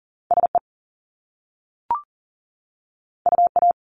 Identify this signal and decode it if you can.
Morse code, heard as a keyed tone, about 41 wpm, 700 Hz. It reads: SE UA
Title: Top Two Letter Combinations in Words 40wpm